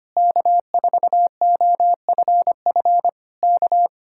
Morse code, heard as a keyed tone, about 24 wpm, 700 Hz. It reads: K4OFF K
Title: Speed-Racing Common QSO Phrases - 25wpm